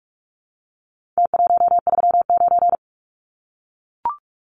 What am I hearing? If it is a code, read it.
Morse code, heard as a keyed tone, about 44 wpm, 700 Hz. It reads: T139
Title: Numbers - Contest Style - 4-Digit 45wpm